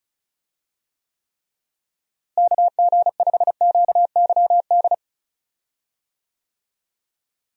Morse code, heard as a keyed tone, about 34 wpm, 700 Hz. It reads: KG5QYD
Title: USA Call Signs - 6 Characters - Repeated 3x - 35wpm